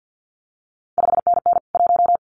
Morse code, heard as a keyed tone, about 50 wpm, 700 Hz. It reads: 5NN 1
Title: CQ WW DX Contest Exchange - 50wpm